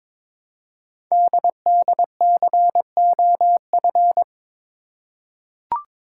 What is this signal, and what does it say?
Morse code, heard as a keyed tone, about 22 wpm, 700 Hz. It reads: DDCOF
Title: Code Groups - 5 Letters - 22wpm